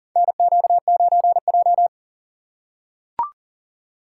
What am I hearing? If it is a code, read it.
Morse code, heard as a keyed tone, about 40 wpm, 700 Hz. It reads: NQ9J
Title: USA Call Signs - 6x Character Spacing - 40wpm